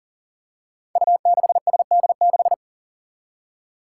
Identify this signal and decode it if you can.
Morse code, heard as a keyed tone, about 39 wpm, 700 Hz. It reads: U6SD6
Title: Five Challenging Letter-Number Combinations 40wpm